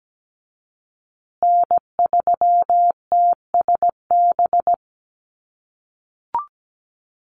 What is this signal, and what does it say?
Morse code, heard as a keyed tone, about 17 wpm, 700 Hz. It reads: N3TSB
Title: USA Call Signs Repeated 2x 17wpm